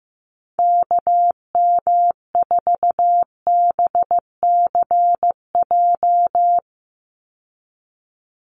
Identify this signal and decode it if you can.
Morse code, heard as a keyed tone, about 15 wpm, 700 Hz. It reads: KM4BCJ
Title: USA Call Signs 15wpm